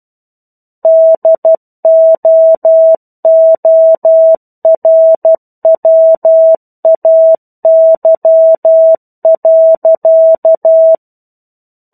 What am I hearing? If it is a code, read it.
Morse code, heard as a keyed tone, about 12 wpm, 650 Hz. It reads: DOORWAY.
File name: 01-Chapter-01 12 WPM